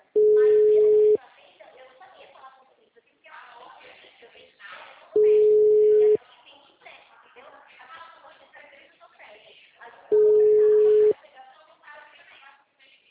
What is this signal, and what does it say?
Ringback tone (Europe, 425 Hz), with speech.